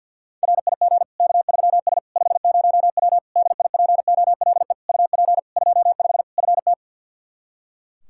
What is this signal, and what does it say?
Morse code, heard as a keyed tone, about 50 wpm, 700 Hz. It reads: RIG K3S 50W DIPOLE UP 25 FT
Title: Practice QSO - Exchange 2 - Info - 50wpm